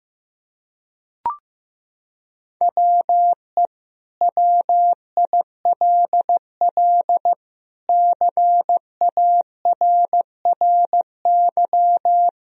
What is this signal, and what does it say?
Morse code, heard as a keyed tone, about 16 wpm, 700 Hz. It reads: WE WILL CARRY
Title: Sentences from Top 1000 Words - Review Part 1 - 15wpm